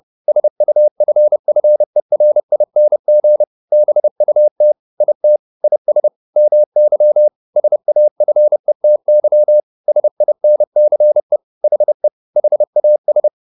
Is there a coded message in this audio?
Yes: Morse code from 0.3 to 13.3 s, heard as a keyed tone, about 30 wpm, 600 Hz. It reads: SUFFERING BUT IT IS MY SAFETY SINCE HE HAS